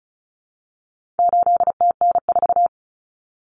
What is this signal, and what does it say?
Morse code, heard as a keyed tone, about 34 wpm, 700 Hz. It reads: 8TN4
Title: Common Cut Numbers - 4-Digit 35wpm